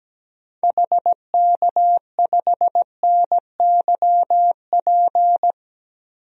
Morse code, heard as a keyed tone, about 17 wpm, 700 Hz. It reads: HK5NYP
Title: DXCC-100 Call Signs - 6 Characters - Repeated 2x - 17wpm